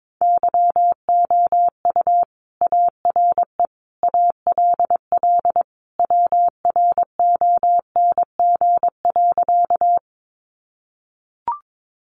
Morse code, heard as a keyed tone, about 22 wpm, 700 Hz. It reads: YOU ARE ALL WRONG.
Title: Sentences from Top 900 Words 22wpm